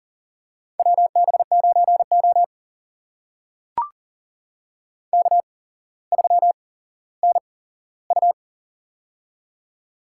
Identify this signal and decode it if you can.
Morse code, heard as a keyed tone, about 40 wpm, spaced as for 17 wpm, 700 Hz. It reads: WB9O K3NU
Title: USA Call Signs - 8x Character Spacing - 40wpm